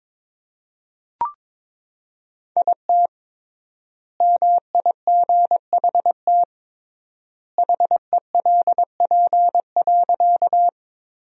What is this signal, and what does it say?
Morse code, heard as a keyed tone, about 22 wpm, 700 Hz. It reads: IT MIGHT HELP.
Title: 3x Word-Spacing - Sentences from Top 300 Words - 22wpm